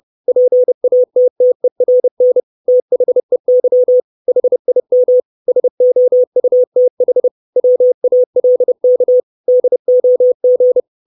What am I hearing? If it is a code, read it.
Morse code, heard as a keyed tone, about 30 wpm, 500 Hz. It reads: PATTERN THEY HIM SOUTH WALK DOG
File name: 632 Words 30WPM